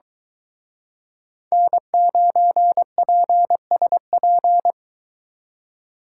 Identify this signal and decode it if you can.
Morse code, heard as a keyed tone, about 23 wpm, 700 Hz. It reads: N9PSP